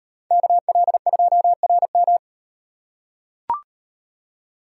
Morse code, heard as a keyed tone, about 38 wpm, 700 Hz. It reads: KL2RM